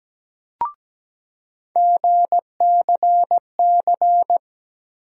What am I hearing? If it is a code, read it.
Morse code, heard as a keyed tone, about 17 wpm, 700 Hz. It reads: GCC
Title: USA Call Sign Postfixes - 3 Letters 17wpm